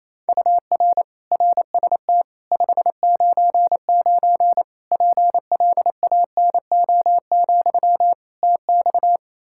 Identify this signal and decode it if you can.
Morse code, heard as a keyed tone, about 28 wpm, 700 Hz. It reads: UR RST 599 PLANO, TX
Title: K1USN SST QSO Practice 28wpm